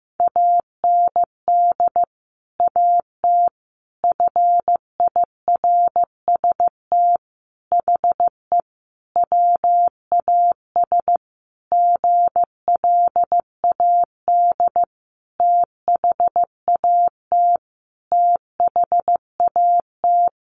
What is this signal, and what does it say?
Morse code, heard as a keyed tone, about 15 wpm, 700 Hz. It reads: AND AT FIRST HE WAS GLAD THAT THAT
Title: Sentences from Top 1300 Words 15wpm